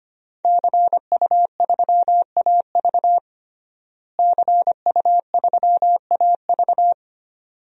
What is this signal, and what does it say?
Morse code, heard as a keyed tone, about 25 wpm, 700 Hz. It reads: CU3AV CU3AV
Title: DXCC-100 Call Signs - 5 Characters - Repeated 2x - 25wpm